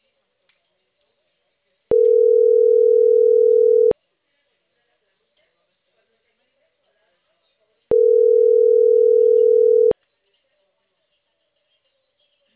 Ringback tone (North America), 440 + 480 Hz.